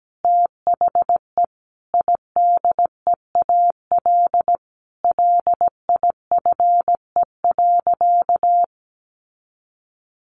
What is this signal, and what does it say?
Morse code, heard as a keyed tone, about 17 wpm, 700 Hz. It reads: THE IDEAL LIFE.
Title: Sentences from Top 1400 Words 17wpm